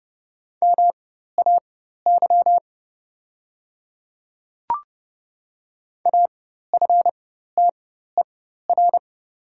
Morse code, heard as a keyed tone, about 30 wpm, spaced as for 14 wpm, 700 Hz. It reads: MAY AFTER
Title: 4x Character Spacing - Top 500 Words - 30wpm